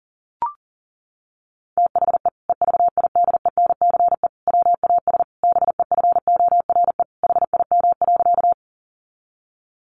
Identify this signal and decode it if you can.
Morse code, heard as a keyed tone, about 40 wpm, 700 Hz. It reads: THE EVIDENCE WAS BEFORE HIM.
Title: Sentences from Top 1300 Words 40wpm